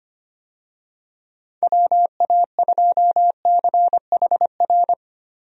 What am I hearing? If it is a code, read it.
Morse code, heard as a keyed tone, about 25 wpm, 700 Hz. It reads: WA2CHR
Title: USA Call Signs - 6 Characters - Repeated 2x - 25wpm